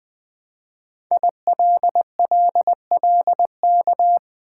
Morse code, heard as a keyed tone, about 20 wpm, 700 Hz. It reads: ILLLK
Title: Code Groups - 5 Letters - 20wpm